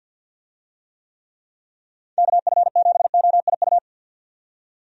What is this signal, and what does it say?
Morse code, heard as a keyed tone, about 50 wpm, 700 Hz. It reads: KF7OIU
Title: USA Call Signs - 6 Characters - 50wpm